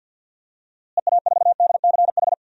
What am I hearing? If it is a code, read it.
Morse code, heard as a keyed tone, about 52 wpm, 700 Hz. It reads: EA4DCH
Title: DXCC-100 Call Signs - 6 Characters - Repeated 2x - 50wpm